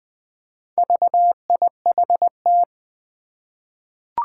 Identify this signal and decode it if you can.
Morse code, heard as a keyed tone, about 20 wpm, 700 Hz. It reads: VIHT
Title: Four Challenging Letter-Number Combinations 20wpm